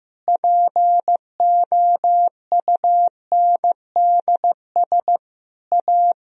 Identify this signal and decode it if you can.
Morse code, heard as a keyed tone, about 15 wpm, 700 Hz. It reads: POUNDS A
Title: Sentences from Top 1000 Words - Review Part 2 - 15wpm